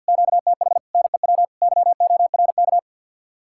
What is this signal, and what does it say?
Morse code, heard as a keyed tone, about 60 wpm, 700 Hz. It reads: QTH NEW YORK
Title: Speed-Racing Common QSO Phrases Encore - 50wpm